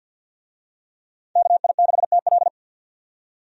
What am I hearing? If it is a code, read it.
Morse code, heard as a keyed tone, about 46 wpm, 700 Hz. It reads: KI6TL
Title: USA Call Signs - 5 Characters - 50wpm